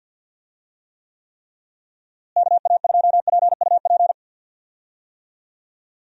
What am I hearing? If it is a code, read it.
Morse code, heard as a keyed tone, about 51 wpm, 700 Hz. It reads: KA2PUP
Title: USA Call Signs - 6 Characters - Repeated 3x - 50wpm